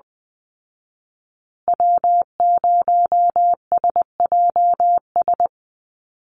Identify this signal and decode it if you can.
Morse code, heard as a keyed tone, about 20 wpm, 700 Hz. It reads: W0SJS
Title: Call Signs Repeated 2x - 20wpm